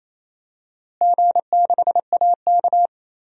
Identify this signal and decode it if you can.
Morse code, heard as a keyed tone, about 27 wpm, 700 Hz. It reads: G6AK
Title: DXCC-100 Call Signs - 4 Characters - Repeated 2x - 28wpm